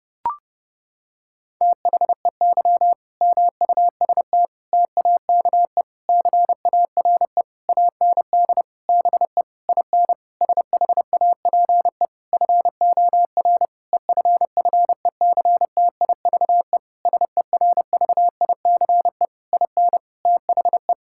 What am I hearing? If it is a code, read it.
Morse code, heard as a keyed tone, about 30 wpm, 700 Hz. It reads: THEY MUST TAKE CARE AND BE IN SHAPE FOR EFFECTIVE SERVICE IN THE
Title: Sentences from Top 1000 Words - Review Part 6 - 30wpm